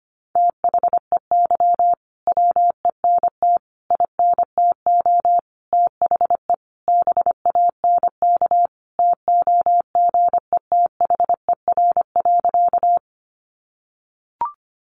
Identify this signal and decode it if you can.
Morse code, heard as a keyed tone, about 25 wpm, 700 Hz. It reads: THEY WENT INTO THE BANK TOGETHER.
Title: Sentences from Top 900 Words 25wpm